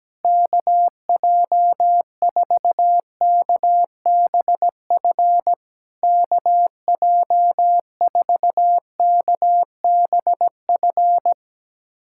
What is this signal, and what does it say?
Morse code, heard as a keyed tone, about 17 wpm, 700 Hz. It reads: KJ4KBF KJ4KBF
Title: SOTA QSO Practice 17wpm